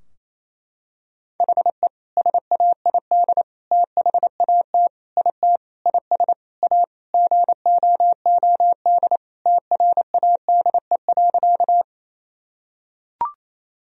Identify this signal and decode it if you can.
Morse code, heard as a keyed tone, about 28 wpm, 700 Hz. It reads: HE SAID THAT IT IS A GOOD TRADE.
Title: Sentences from Top 1000 Words - Review Part 4 - 28wpm